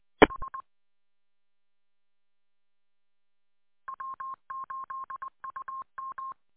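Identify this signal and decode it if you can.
Morse code, heard as a keyed tone, about 20 wpm, 1070 Hz. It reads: H W8UM
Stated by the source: Morse code from station W8UM on 145.230.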